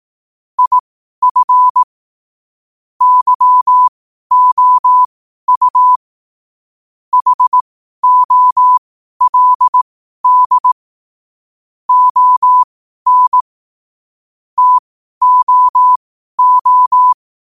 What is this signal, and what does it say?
Morse code, heard as a keyed tone, about 18 wpm, spaced as for 12 wpm, 1000 Hz. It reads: IF YOU HOLD ON TOO
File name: qotdfiles / 12wpm